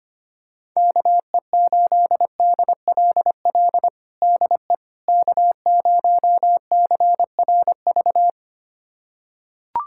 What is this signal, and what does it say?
Morse code, heard as a keyed tone, about 25 wpm, 700 Hz. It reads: KE8DLL DE K0CRV
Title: USA Callsign DE Callsign 25wpm